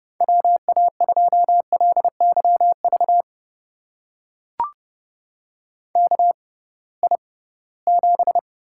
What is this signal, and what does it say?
Morse code, heard as a keyed tone, about 32 wpm, spaced as for 20 wpm, 700 Hz. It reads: WA2LYV KI7
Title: USA Call Signs - 6 Characters - 6x Character Spacing - 30wpm